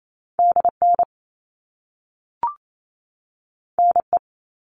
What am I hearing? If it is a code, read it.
Morse code, heard as a keyed tone, about 28 wpm, 700 Hz. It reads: DN NE
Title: All Cut Numbers - 2-Digit 28wpm